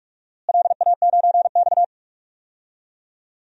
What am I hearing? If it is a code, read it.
Morse code, heard as a keyed tone, about 47 wpm, 700 Hz. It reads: LA9X
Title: DXCC-100 Call Signs - 4 Characters - Repeated 2x - 45wpm